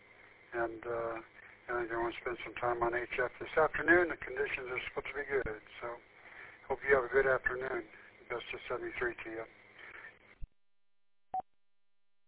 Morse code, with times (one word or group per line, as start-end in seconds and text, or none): none